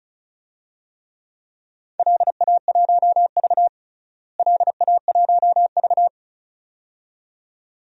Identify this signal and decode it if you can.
Morse code, heard as a keyed tone, about 36 wpm, 700 Hz. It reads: LA1V LA1V
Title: DXCC-100 Call Signs - 4 Characters - Repeated 2x - 35wpm